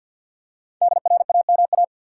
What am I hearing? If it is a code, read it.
Morse code, heard as a keyed tone, about 52 wpm, 700 Hz. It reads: DRAMA
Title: Five Letter Words - 50wpm